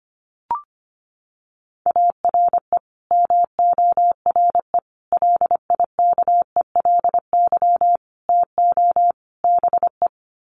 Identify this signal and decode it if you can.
Morse code, heard as a keyed tone, about 24 wpm, 700 Hz. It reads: ARE MORE LIKELY TO BE
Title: Sets of 5 Words 25wpm